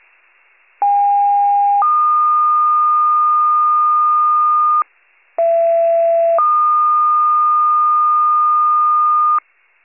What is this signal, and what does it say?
Two-tone page, heard as a one-second tone, then a three-second tone, about 810 then 1220 Hz, about 670 then 1220 Hz. It is heard from 0.8 to 9.4 s.